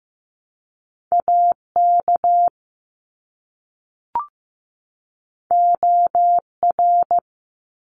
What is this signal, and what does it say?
Morse code, heard as a keyed tone, about 15 wpm, 700 Hz. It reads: AK OR
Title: ARRL Sections 15wpm